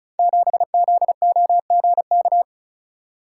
Morse code, heard as a keyed tone, about 36 wpm, 700 Hz. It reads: 7ZOGK
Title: Code Groups - 5 Letter:Numbers - 35wpm